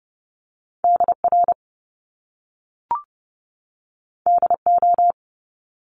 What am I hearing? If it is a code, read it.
Morse code, heard as a keyed tone, about 30 wpm, 700 Hz. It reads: DR DO
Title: Top Two Letter Combinations in Words 30wpm